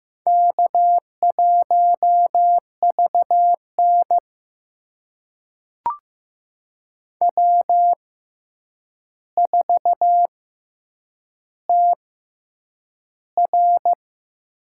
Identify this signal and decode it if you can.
Morse code, heard as a keyed tone, about 15 wpm, spaced as for 8 wpm, 700 Hz. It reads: K1VN W4TR
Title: USA Call Signs - 4 Characters - 6x Character Spacing - 15wpm